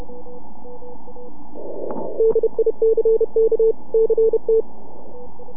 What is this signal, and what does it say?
Morse code, heard as a keyed tone, about 30 wpm, 450 Hz. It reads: DICK CT
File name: NAQPCW-8-2009 / W1UJ NAQP CW 09 1800Z